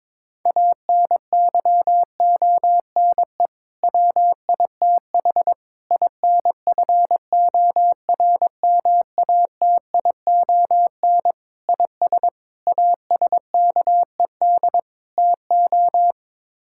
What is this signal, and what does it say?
Morse code, heard as a keyed tone, about 22 wpm, 700 Hz. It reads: ANYONE WITH INFORMATION IS ASKED TO
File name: practice / 22wpm / Sets of 6 Words - 22wpm